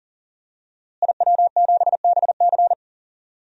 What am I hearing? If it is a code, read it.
Morse code, heard as a keyed tone, about 40 wpm, 700 Hz. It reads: IW7BC
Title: DXCC-100 Call Signs - 5 Characters - Repeated 3x - 40wpm